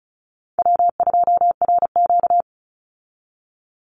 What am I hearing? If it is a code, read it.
Morse code, heard as a keyed tone, about 34 wpm, 700 Hz. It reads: W2RQ
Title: Easy Call Signs Repeated 3x - 35wpm